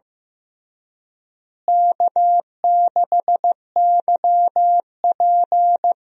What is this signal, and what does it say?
Morse code, heard as a keyed tone, about 16 wpm, 700 Hz. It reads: K6YP
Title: USA Call Signs - 4 Characters - Repeated 2x - 15wpm